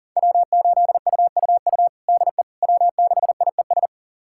Morse code, heard as a keyed tone, about 40 wpm, 700 Hz. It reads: W8UUU DE W6IES
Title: USA Callsign DE Callsign - 5 Characters - 40wpm